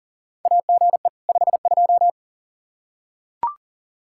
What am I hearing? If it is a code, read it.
Morse code, heard as a keyed tone, about 40 wpm, 700 Hz. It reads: AGE 52
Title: Practice QSO - Exchange 3 - More Info - 40wpm